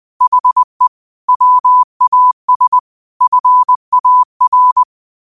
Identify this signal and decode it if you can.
Morse code, heard as a keyed tone, about 20 wpm, 1000 Hz. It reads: HE WAS FAR
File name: qotdfiles / 20wpm